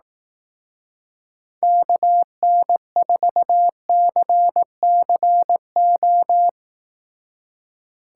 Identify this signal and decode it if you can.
Morse code, heard as a keyed tone, about 17 wpm, 700 Hz. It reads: KN4CCO
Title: Speed Racing - USA Call Signs 15wpm